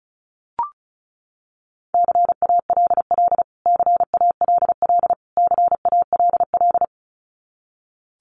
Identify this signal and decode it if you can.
Morse code, heard as a keyed tone, about 34 wpm, 700 Hz. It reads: CALL CALL CALL
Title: Top 200 Words Repeated 35wpm